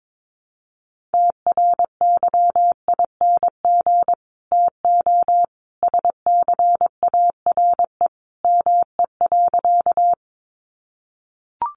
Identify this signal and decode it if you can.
Morse code, heard as a keyed tone, about 22 wpm, 700 Hz. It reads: TRYING TO SCARE ME.
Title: A Sound Of Thunder 22wpm